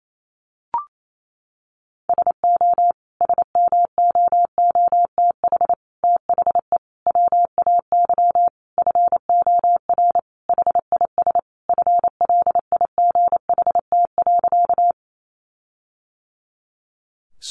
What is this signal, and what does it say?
Morse code, heard as a keyed tone, about 28 wpm, 700 Hz. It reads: SO SMOOTH THE WAY FOR HIS FLIGHT.
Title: Sentences from Top 1400 Words 28wpm